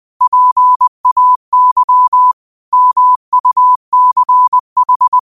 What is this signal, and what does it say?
Morse code, heard as a keyed tone, about 20 wpm, 1000 Hz. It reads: PAY MUCH
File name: qotdfiles / 20wpm